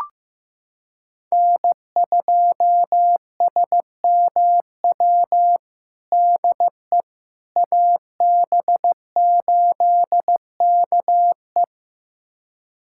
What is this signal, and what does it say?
Morse code, heard as a keyed tone, about 15 wpm, 700 Hz. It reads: N2SMW DE AB8KE
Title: USA Callsign DE Callsign - 5 Characters - 15wpm